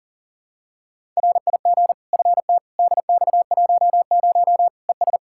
Morse code, heard as a keyed tone, about 37 wpm, 700 Hz. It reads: RIG FT DX10 ES
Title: Practice QSO - Exchange 2 - Info - 40wpm